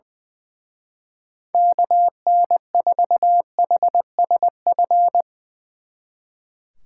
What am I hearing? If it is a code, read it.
Morse code, heard as a keyed tone, about 22 wpm, 700 Hz. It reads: KN4HSF
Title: Speed Racing - USA 2x3 Call Signs 20wpm